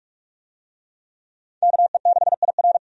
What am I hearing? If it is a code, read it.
Morse code, heard as a keyed tone, about 46 wpm, 700 Hz. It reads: KE6IR
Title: USA Call Signs - 5 Characters - 45wpm